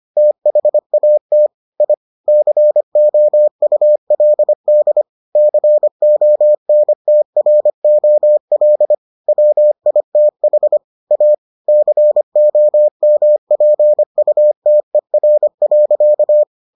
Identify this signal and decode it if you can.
Morse code, heard as a keyed tone, about 25 wpm, 600 Hz. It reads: THAT I COULD CONTROL WITH A COMPUTER.